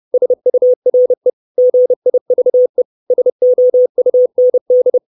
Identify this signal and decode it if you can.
Morse code, heard as a keyed tone, about 30 wpm, 500 Hz. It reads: SURE GIVE SOUND
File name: 632 Words 30WPM